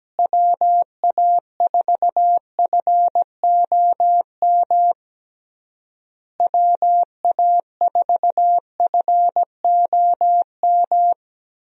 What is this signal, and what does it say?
Morse code, heard as a keyed tone, about 17 wpm, 700 Hz. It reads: WA4FOM WA4FOM
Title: USA Call Signs - 6 Characters - Repeated 3x - 17wpm